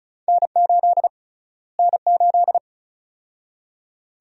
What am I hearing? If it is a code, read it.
Morse code, heard as a keyed tone, about 34 wpm, 700 Hz. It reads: N8 N8
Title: USA Call Sign Prefixes - Repeated 2x - 35wpm